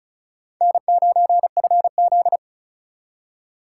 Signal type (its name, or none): Morse code